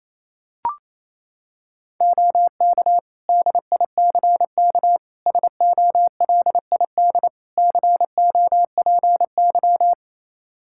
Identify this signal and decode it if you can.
Morse code, heard as a keyed tone, about 28 wpm, 700 Hz. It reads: OK DICK SOLID COPY
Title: Practice QSO - Exchange 3 - More Info - 28wpm